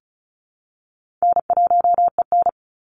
Morse code, heard as a keyed tone, about 34 wpm, 700 Hz. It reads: N1EN
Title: Call Signs - 35wpm